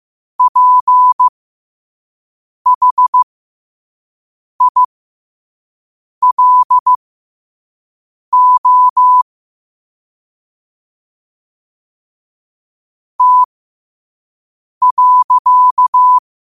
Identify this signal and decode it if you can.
Morse code, heard as a keyed tone, about 15 wpm, spaced as for 5 wpm, 1000 Hz. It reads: PHILO T.
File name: qotdfiles / 5wpm